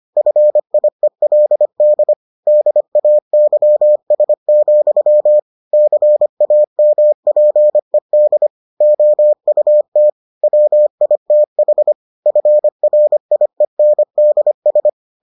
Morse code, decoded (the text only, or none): FIELD DAYS, CAMPED OUT WITH FRIENDS